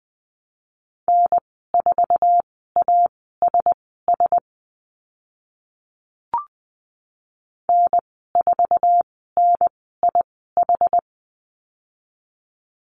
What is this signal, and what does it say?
Morse code, heard as a keyed tone, about 20 wpm, spaced as for 14 wpm, 700 Hz. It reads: N4ASS N4NIH
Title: 062 Lesson 11g - Letter h - Call Signs 20wpm